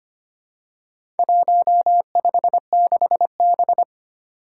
Morse code, heard as a keyed tone, about 25 wpm, 700 Hz. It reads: 156B
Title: Four Challenging Letter-Number Combinations 25wpm